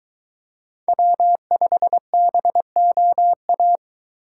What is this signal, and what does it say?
Morse code, heard as a keyed tone, about 22 wpm, 700 Hz. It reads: W5BOA